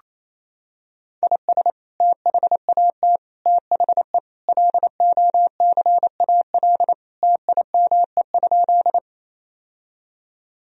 Morse code, heard as a keyed tone, about 28 wpm, 700 Hz. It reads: IS THAT THE LOCAL TIME?
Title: Sentences from Top 1000 Words - Review Part 3 - 28wpm